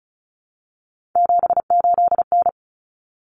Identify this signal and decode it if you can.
Morse code, heard as a keyed tone, about 34 wpm, 700 Hz. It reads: 78N
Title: Common Cut Numbers - 3-Digit 35wpm